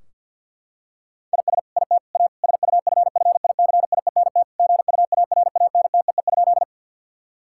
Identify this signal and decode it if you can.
Morse code, heard as a keyed tone, about 50 wpm, 700 Hz. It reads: IS IT A SUFFICIENT GUARANTEE?
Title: Sentences from Top 1600 Words 50wpm